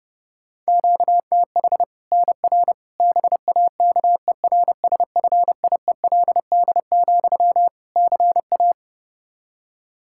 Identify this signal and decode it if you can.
Morse code, heard as a keyed tone, about 32 wpm, 700 Hz. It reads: QTH NR BAKERSFIELD, CA
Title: K1USN SST QSO Practice 30wpm